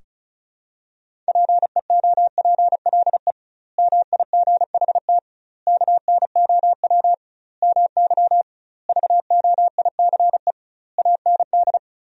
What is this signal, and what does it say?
Morse code, heard as a keyed tone, about 34 wpm, 700 Hz. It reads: PEOPLE MIGHT KNOW MY VOICE AND
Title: 2x Word-Spacing - Sentences from Top 500 Words - 35wpm